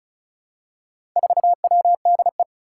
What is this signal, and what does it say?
Morse code, heard as a keyed tone, about 36 wpm, 700 Hz. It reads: 4WDE
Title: Four Challenging Letter-Number Combinations 35wpm